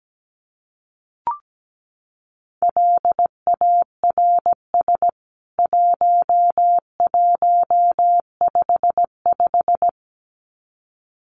Morse code, heard as a keyed tone, about 17 wpm, 700 Hz. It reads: LARS 1155
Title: CWT Exchange 17wpm